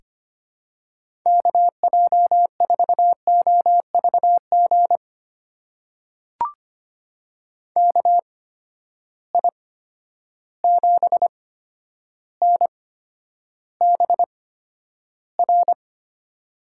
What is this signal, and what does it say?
Morse code, heard as a keyed tone, about 25 wpm, spaced as for 11 wpm, 700 Hz. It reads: KJ4OVG KI7NBR
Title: USA Call Signs - 6 Characters - 8x Character Spacing - 25wpm